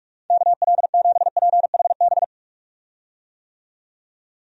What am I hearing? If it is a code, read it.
Morse code, heard as a keyed tone, about 46 wpm, 700 Hz. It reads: KL7PHB
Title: DXCC-100 Call Signs - 6 Characters - 45wpm